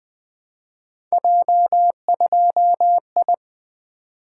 Morse code, heard as a keyed tone, about 20 wpm, 700 Hz. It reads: J2I
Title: Three Letter-Number Combinations - 20wpm